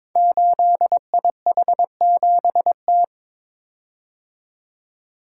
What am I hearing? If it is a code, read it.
Morse code, heard as a keyed tone, about 22 wpm, 700 Hz. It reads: 8IH7T
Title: Code Groups - 5 Letter:Numbers - 22wpm